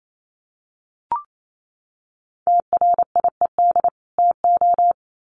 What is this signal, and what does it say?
Morse code, heard as a keyed tone, about 28 wpm, 700 Hz. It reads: TRIED TO
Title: Sets of 2 Words 28wpm